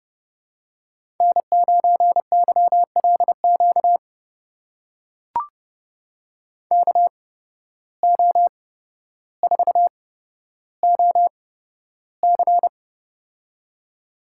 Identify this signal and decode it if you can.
Morse code, heard as a keyed tone, about 32 wpm, spaced as for 13 wpm, 700 Hz. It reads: N9YLQ KO4OC
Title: USA Call Signs - 5 Characters - 8x Character Spacing - 30wpm